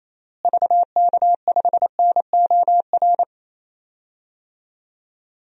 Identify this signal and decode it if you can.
Morse code, heard as a keyed tone, about 28 wpm, 700 Hz. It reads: VK5NOR
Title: DXCC-100 Call Signs - 6 Characters - Repeated 3x - 28wpm